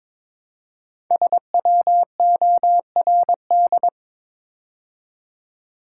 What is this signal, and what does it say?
Morse code, heard as a keyed tone, about 22 wpm, 700 Hz. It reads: SWORD
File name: 5 Letter Words - First 3 Letters Given - 22wpm